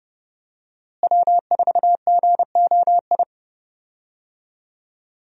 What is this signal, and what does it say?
Morse code, heard as a keyed tone, about 30 wpm, 700 Hz. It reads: W4GOI
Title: USA Call Signs - 5 Characters - Repeated 3x - 30wpm